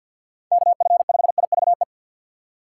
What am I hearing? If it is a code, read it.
Morse code, heard as a keyed tone, about 53 wpm, 700 Hz. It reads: KF5IVE
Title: USA Call Signs - 6 Characters - 6x Character Spacing - 50wpm